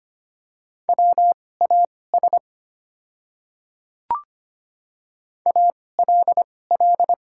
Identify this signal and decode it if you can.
Morse code, heard as a keyed tone, about 25 wpm, spaced as for 18 wpm, 700 Hz. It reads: WAS ALL
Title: 2x Character Spacing - Top 200 Words - 25wpm